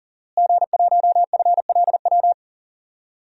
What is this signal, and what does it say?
Morse code, heard as a keyed tone, about 38 wpm, 700 Hz. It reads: G1FLW
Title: DXCC-100 Call Signs - 5 Characters - Repeated 3x - 40wpm